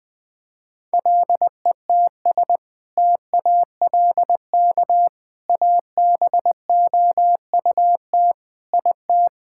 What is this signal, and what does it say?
Morse code, heard as a keyed tone, about 21 wpm, 700 Hz. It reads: LETS TALK ABOUT IT
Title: Sentences with 2-Word Phrases - 20wpm